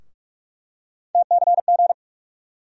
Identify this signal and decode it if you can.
Morse code, heard as a keyed tone, about 43 wpm, 700 Hz. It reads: TCG
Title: USA Call Sign Postfixes - 3 Letters 45wpm